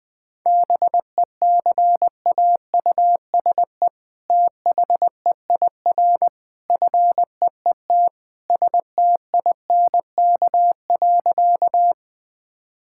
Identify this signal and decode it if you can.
Morse code, heard as a keyed tone, about 20 wpm, 700 Hz. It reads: BECAUSE THEIR FEET STINK.
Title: All-Age Appropriate Jokes - 20wpm